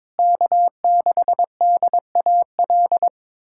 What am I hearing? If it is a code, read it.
Morse code, heard as a keyed tone, about 21 wpm, 700 Hz. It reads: K6DAL
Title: USA Callsign DE Callsign - 5 Characters - 22wpm